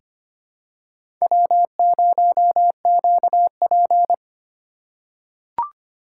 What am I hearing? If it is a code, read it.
Morse code, heard as a keyed tone, about 25 wpm, 700 Hz. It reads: W0QP
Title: USA Call Signs - 4 Characters - 3x Character Spacing - 25wpm